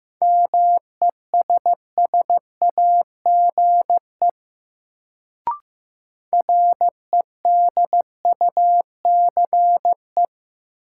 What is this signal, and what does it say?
Morse code, heard as a keyed tone, about 15 wpm, 700 Hz. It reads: MESSAGE REDUCE
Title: Sentences from Top 1100 Words - New Word Drill Alt - 15wpm